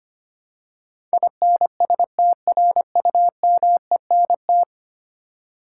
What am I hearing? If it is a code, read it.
Morse code, heard as a keyed tone, about 25 wpm, 700 Hz. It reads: INSTRUMENT
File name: practice / 25wpm / Ten Letter Words - 25wpm